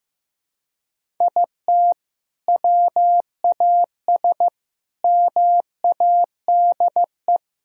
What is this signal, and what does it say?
Morse code, heard as a keyed tone, about 15 wpm, 700 Hz. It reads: IT WAS MADE
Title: Sentences from Top 1000 Words - Review Part 4 - 15wpm